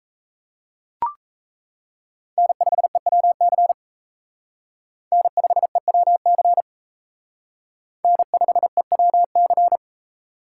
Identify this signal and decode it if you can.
Morse code, heard as a keyed tone, about 34 wpm, 700 Hz. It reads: N5EWC N5EWC N5EWC
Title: Speed Racing - USA 1x3 Call Signs 28wpm